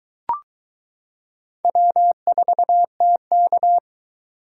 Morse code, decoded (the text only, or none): W4TK